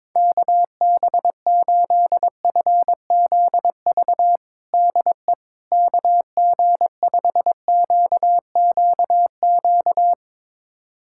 Morse code, heard as a keyed tone, about 22 wpm, 700 Hz. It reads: KB8FZV DE KG5QQQ
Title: USA Callsign DE Callsign - 6 Characters - 22wpm